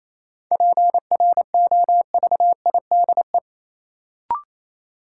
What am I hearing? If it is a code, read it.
Morse code, heard as a keyed tone, about 28 wpm, 700 Hz. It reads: PROVIDE
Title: Sentences from Top 1000 Words - Review Part 2 - New Word Drill - 28wpm